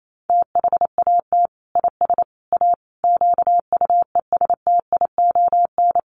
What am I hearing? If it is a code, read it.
Morse code, heard as a keyed tone, about 28 wpm, 700 Hz. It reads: THAT IS A QUESTION
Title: Sentences from Top 600 Words 28wpm